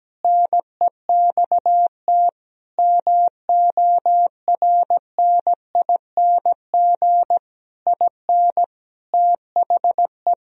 Morse code, heard as a keyed tone, about 17 wpm, 700 Hz. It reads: NEXT MORNING IN THE
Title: Sentences from Top 1000 Words - Review Part 3 - 17wpm